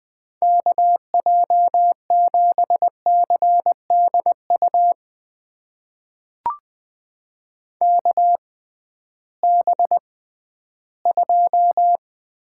Morse code, heard as a keyed tone, about 20 wpm, spaced as for 13 wpm, 700 Hz. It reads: KJ7CDU KB2
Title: USA Call Signs - 6 Characters - 6x Character Spacing - 20wpm